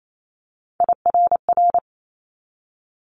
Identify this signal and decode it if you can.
Morse code, heard as a keyed tone, about 28 wpm, 700 Hz. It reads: IRR
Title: Multi-letter Phonemes 28wpm